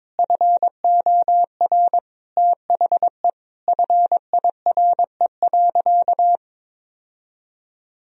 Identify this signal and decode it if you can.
Morse code, heard as a keyed tone, about 22 wpm, 700 Hz. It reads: FOR THE FIRE.
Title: Sentences from Top 1000 Words - Review Part 2 - 22wpm